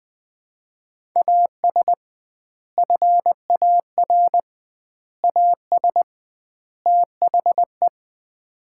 Morse code, heard as a keyed tone, about 20 wpm, 700 Hz. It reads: AS FAR AS THE
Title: 2x Word-Spacing - Sets of 4 Words - 20wpm